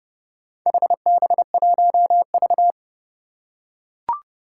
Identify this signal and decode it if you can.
Morse code, heard as a keyed tone, about 30 wpm, 700 Hz. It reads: HB1V
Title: Four Challenging Letter-Number Combinations 30wpm